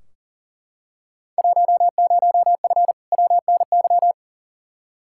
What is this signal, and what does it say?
Morse code, heard as a keyed tone, about 37 wpm, 700 Hz. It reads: 10F WNY
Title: Field Day Exchanges Encore - 40wpm